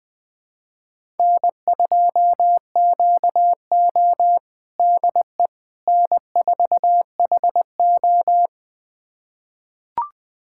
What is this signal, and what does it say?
Morse code, heard as a keyed tone, about 20 wpm, 700 Hz. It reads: N2QO DE N4HO
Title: USA Callsign DE Callsign - 4 Characters - 20wpm